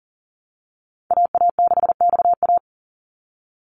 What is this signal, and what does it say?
Morse code, heard as a keyed tone, about 40 wpm, 700 Hz. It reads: AA6XA
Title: Call Signs - 40wpm